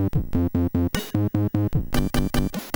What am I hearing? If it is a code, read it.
There is no code.